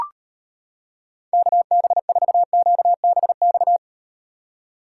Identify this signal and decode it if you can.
Morse code, heard as a keyed tone, about 38 wpm, 700 Hz. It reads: KB4QBX